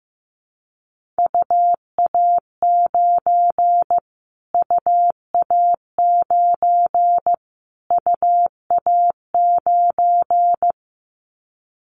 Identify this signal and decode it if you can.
Morse code, heard as a keyed tone, about 15 wpm, 700 Hz. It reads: UA9 UA9 UA9
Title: Callsign Prefixes for Top 25 DXCC - Repeated 3x - 15wpm